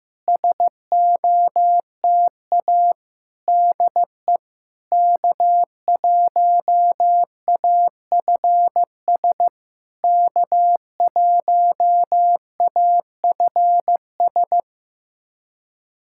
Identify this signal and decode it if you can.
Morse code, heard as a keyed tone, about 15 wpm, 700 Hz. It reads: SOTA DE K1AFS K1AFS
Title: SOTA QSO Practice 15wpm